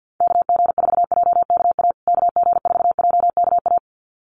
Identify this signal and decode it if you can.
Morse code, heard as a keyed tone, about 48 wpm, 700 Hz. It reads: KG4JKA KG4JKA
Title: USA Call Signs Repeated 2x 50wpm